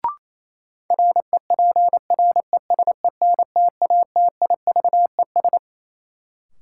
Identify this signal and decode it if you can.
Morse code, heard as a keyed tone, about 28 wpm, 700 Hz. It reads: REPRESENTATIVES
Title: Sets of 5 Words Encore - New Word Drill - 28wpm